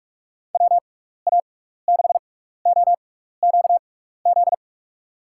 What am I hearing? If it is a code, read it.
Morse code, heard as a keyed tone, about 44 wpm, spaced as for 15 wpm, 700 Hz. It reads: WA6OQZ
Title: USA Call Signs - 6 Characters - 6x Character Spacing - 45wpm